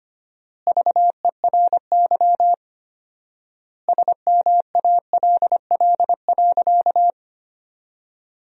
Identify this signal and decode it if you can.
Morse code, heard as a keyed tone, about 25 wpm, 700 Hz. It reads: VERY SMALL.
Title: 4x Word-Spacing - Sentences from Top 400 Words - 25wpm